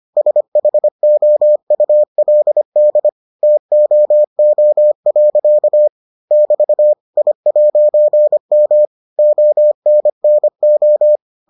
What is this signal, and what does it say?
Morse code, heard as a keyed tone, about 25 wpm, 600 Hz. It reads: SHOULD TOO. <BT> I'M ONNO